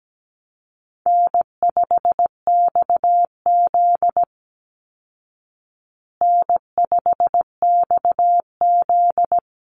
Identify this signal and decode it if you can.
Morse code, heard as a keyed tone, about 17 wpm, 700 Hz. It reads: N5XZ N5XZ
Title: Call Signs Repeated 3x - 17wpm